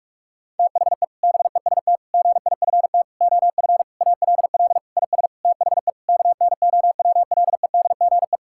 Morse code, heard as a keyed tone, about 46 wpm, 700 Hz. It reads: THE BEST GIFT OF ALL IS THE KNOWLEDGE
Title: Sentences from Top 1000 Words - Review Part 6 - 45wpm